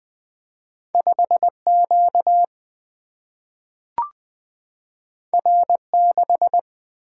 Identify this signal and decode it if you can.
Morse code, heard as a keyed tone, about 20 wpm, 700 Hz. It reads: 5Q R6
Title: Single-Letter Single-Number 20wpm